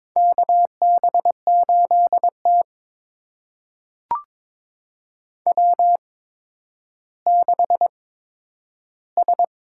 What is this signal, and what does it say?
Morse code, heard as a keyed tone, about 21 wpm, spaced as for 11 wpm, 700 Hz. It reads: KB8T W6S
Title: USA Call Signs - 4 Characters - 8x Character Spacing - 22wpm